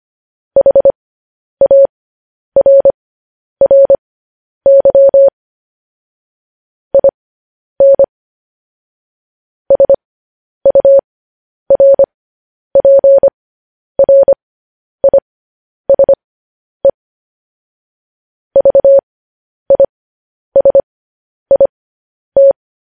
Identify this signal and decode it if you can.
Morse code, heard as a keyed tone, about 25 wpm, spaced as for 10 wpm, 550 Hz. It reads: HARRY IN SURPRISE VISIT
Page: news headlines in morse code at 10 wpm